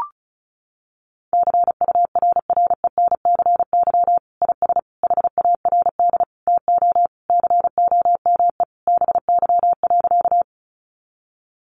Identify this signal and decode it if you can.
Morse code, heard as a keyed tone, about 34 wpm, 700 Hz. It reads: CURRENCY IS HARD TO COME BY.
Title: Sentences from Top 1500 Words 35wpm